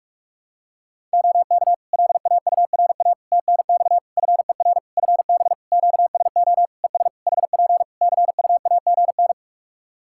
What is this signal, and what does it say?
Morse code, heard as a keyed tone, about 46 wpm, 700 Hz. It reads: OK LAURA TNX FER FB QSO ES HP CUAGN
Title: K1USN SST QSO Practice 45wpm